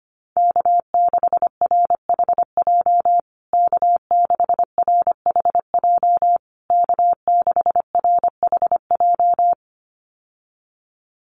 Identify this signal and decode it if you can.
Morse code, heard as a keyed tone, about 25 wpm, 700 Hz. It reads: K6RHJ K6RHJ K6RHJ
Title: USA Call Signs Repeated 3x 25wpm